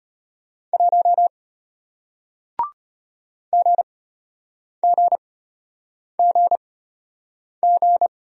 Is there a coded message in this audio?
Yes: Morse code from 0.7 to 8.1 s, heard as a keyed tone, about 34 wpm, spaced as for 8 wpm, 700 Hz. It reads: 1 GGGG